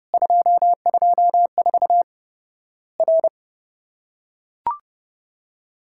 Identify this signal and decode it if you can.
Morse code, heard as a keyed tone, about 30 wpm, 700 Hz. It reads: 224 R
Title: Worked All Europe DX Contest - QTC Exchange - 30wpm